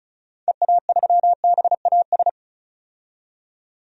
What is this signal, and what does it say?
Morse code, heard as a keyed tone, about 36 wpm, 700 Hz. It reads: EA3BAS
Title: DXCC-100 Call Signs - 6 Characters - 35wpm